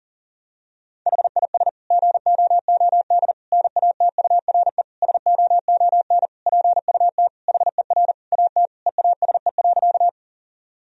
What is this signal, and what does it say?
Morse code, heard as a keyed tone, about 40 wpm, 700 Hz. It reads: HIS GOOD NATURE SOON PUT HER AT EASE.
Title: Sentences from Top 1000 Words - Review Part 1 - 40wpm